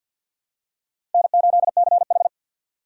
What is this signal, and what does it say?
Morse code, heard as a keyed tone, about 53 wpm, 700 Hz. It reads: N8CH
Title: USA Call Signs - 4 Characters - 6x Character Spacing - 50wpm